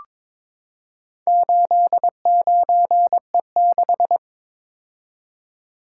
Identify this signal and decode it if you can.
Morse code, heard as a keyed tone, about 22 wpm, 700 Hz. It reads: 89E6
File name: practice / 22wpm / Four Letter-Number Combinations - 22wpm